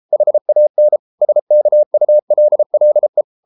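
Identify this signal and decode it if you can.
Morse code, heard as a keyed tone, about 33 wpm, 600 Hz. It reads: HAN SKULLE
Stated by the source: Morse code at 33 wpm, QRQ.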